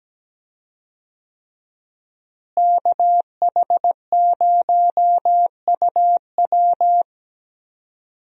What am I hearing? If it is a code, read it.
Morse code, heard as a keyed tone, about 17 wpm, 700 Hz. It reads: KH0UW
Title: USA Call Signs - 5 Characters - Repeated 3x - 17wpm